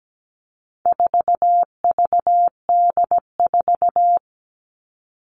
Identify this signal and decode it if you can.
Morse code, heard as a keyed tone, about 17 wpm, 700 Hz. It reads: 4VD4
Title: All Cut Numbers - 4-Digit 17wpm